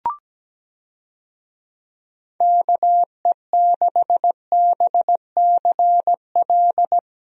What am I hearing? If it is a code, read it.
Morse code, heard as a keyed tone, about 17 wpm, 700 Hz. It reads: KE6BCL
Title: USA Call Signs - 6 Characters - 17wpm